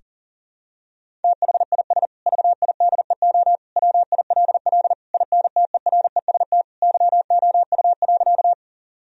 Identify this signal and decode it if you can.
Morse code, heard as a keyed tone, about 40 wpm, 700 Hz. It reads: THIS VIDEO WILL INTEREST YOU.
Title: Sentences from Top 1000 Words - Review Part 4 - 40wpm